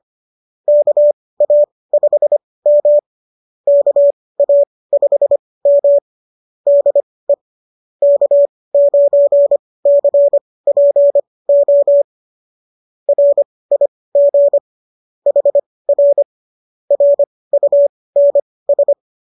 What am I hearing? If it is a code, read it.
Morse code, heard as a keyed tone, about 25 wpm, spaced as for 18 wpm, 570 Hz. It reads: KA5M KA5M DE K9CPO RIG HR RUNS